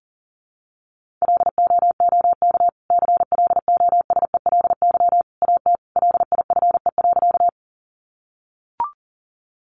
Morse code, heard as a keyed tone, about 40 wpm, 700 Hz. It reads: LOOK CLOSELY AT LIFE.
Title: Sentences from Top 1400 Words 40wpm